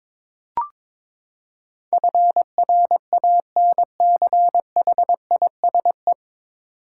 Morse code, heard as a keyed tone, about 22 wpm, 700 Hz. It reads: FRANCHISE
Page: Morse Code Ninja practice 22wpm